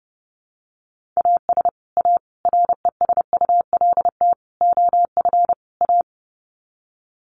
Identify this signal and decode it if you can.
Morse code, heard as a keyed tone, about 30 wpm, 700 Hz. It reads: AS A RESULT OF A
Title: Sets of 5 Words Encore 30wpm